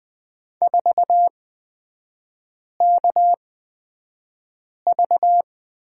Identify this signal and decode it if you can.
Morse code, heard as a keyed tone, about 20 wpm, spaced as for 5 wpm, 700 Hz. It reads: 4KV
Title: 255 Lesson 41e - Period - Three Characters 20wpm